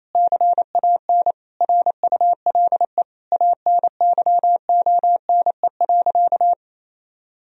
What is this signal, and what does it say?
Morse code, heard as a keyed tone, about 28 wpm, 700 Hz. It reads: CAN RULE ANYONE.
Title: Sentences from Top 1000 Words - Review Part 2 - 28wpm